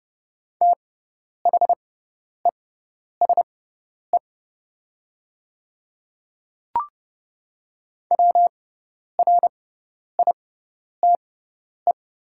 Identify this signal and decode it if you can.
Morse code, heard as a keyed tone, about 30 wpm, spaced as for 10 wpm, 700 Hz. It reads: THESE WRITE
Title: 6x Character Spacing - Top 200 Words - 30wpm